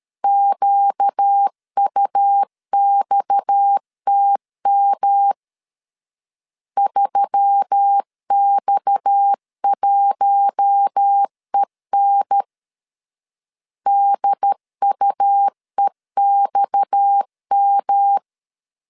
Morse code, heard as a keyed tone, about 13 wpm, 800 Hz. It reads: QUXTM 3X1EN DUEXM
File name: LEZIONI / 6 PKX0 / 6 PKX 12wpm 2024